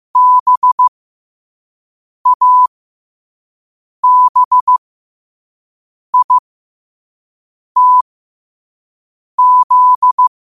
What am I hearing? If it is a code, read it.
Morse code, heard as a keyed tone, about 15 wpm, spaced as for 5 wpm, 1000 Hz. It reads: BABITZ
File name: qotdfiles / 5wpm